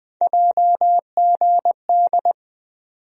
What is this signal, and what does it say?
Morse code, heard as a keyed tone, about 21 wpm, 700 Hz. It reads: JGD
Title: USA Call Sign Postfixes - 3 Letters 20wpm